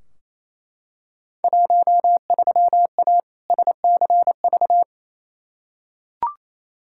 Morse code, heard as a keyed tone, about 28 wpm, 700 Hz. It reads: 13A SCV
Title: Field Day Exchanges Encore - 28wpm